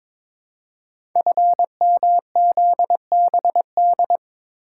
Morse code, heard as a keyed tone, about 22 wpm, 700 Hz. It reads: FMZBD
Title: Code Groups - 5 Letters - 22wpm